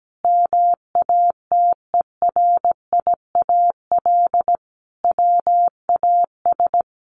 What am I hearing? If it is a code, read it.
Morse code, heard as a keyed tone, about 17 wpm, 700 Hz. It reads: MATERIAL WAS
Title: Sentences from Top 1200 Words 17wpm